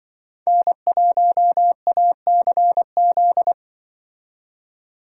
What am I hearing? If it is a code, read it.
Morse code, heard as a keyed tone, about 25 wpm, 700 Hz. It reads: N1ACZ
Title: Speed Racing - USA 1x3 Call Signs 20wpm